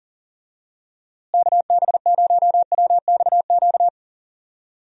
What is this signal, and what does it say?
Morse code, heard as a keyed tone, about 38 wpm, 700 Hz. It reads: KB0WXQ
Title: USA Call Signs - 6 Characters - Repeated 2x - 40wpm